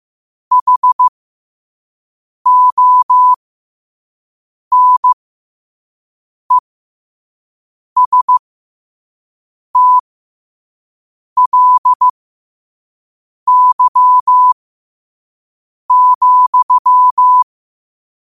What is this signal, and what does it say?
Morse code, heard as a keyed tone, about 15 wpm, spaced as for 5 wpm, 1000 Hz. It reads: HONESTLY,
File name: qotdfiles / 5wpm